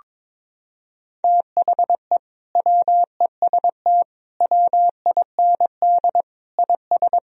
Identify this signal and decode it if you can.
Morse code, heard as a keyed tone, about 22 wpm, 700 Hz. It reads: THE WEST WIND IS
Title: Sentences from Top 1000 Words - Review Part 2 - 22wpm